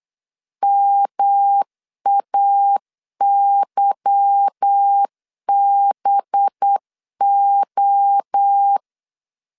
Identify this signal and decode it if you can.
Morse code, heard as a keyed tone, about 8 wpm, 800 Hz. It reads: MAYBO
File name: LEZIONI / 8 YJ45 / 8 Y 8wpm 3245